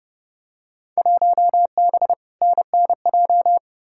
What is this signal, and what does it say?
Morse code, heard as a keyed tone, about 30 wpm, 700 Hz. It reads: 1B NNJ
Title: Field Day Exchanges Encore - 30wpm